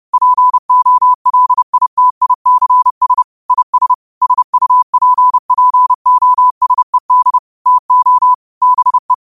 Morse code, heard as a keyed tone, about 30 wpm, 1000 Hz. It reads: POLITICS IS SUPPOSED TO BE